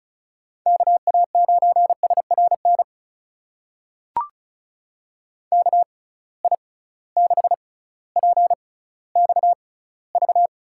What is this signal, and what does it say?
Morse code, heard as a keyed tone, about 36 wpm, spaced as for 18 wpm, 700 Hz. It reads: KA9SRN KI6PXV
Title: USA Call Signs - 6 Characters - 6x Character Spacing - 35wpm